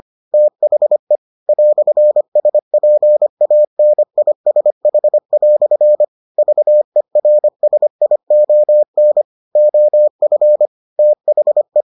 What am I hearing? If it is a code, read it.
Morse code, heard as a keyed tone, about 25 wpm, 600 Hz. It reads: THE "SPANISH" VERSION OF THE